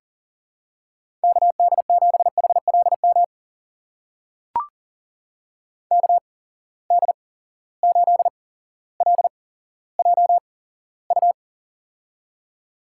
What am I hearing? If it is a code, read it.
Morse code, heard as a keyed tone, about 40 wpm, spaced as for 17 wpm, 700 Hz. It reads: KD7HLM KD8LJU
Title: USA Call Signs - 8x Character Spacing - 40wpm